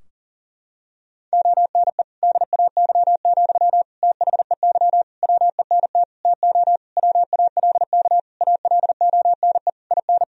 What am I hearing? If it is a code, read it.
Morse code, heard as a keyed tone, about 40 wpm, 700 Hz. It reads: ONE DAY, THEY WENT TO WALK ALONE IN
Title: Sentences from Top 1000 Words - Review Part 2 - 40wpm